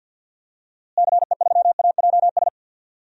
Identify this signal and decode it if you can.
Morse code, heard as a keyed tone, about 53 wpm, 700 Hz. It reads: CE3AJS
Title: DXCC-100 Call Signs - 6 Characters - Repeated 3x - 50wpm